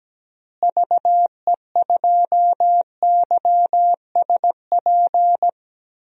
Morse code, heard as a keyed tone, about 17 wpm, 700 Hz. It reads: VE2YSP
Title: DXCC-100 Call Signs - 6 Characters - Repeated 2x - 17wpm